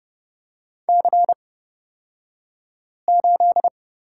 Morse code, heard as a keyed tone, about 31 wpm, spaced as for 5 wpm, 700 Hz. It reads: C8
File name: Lesson 38d - Letter q - Two Characters 30wpm